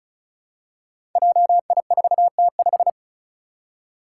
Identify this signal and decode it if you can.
Morse code, heard as a keyed tone, about 36 wpm, 700 Hz. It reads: JI4T5
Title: Five Challenging Letter-Number Combinations 35wpm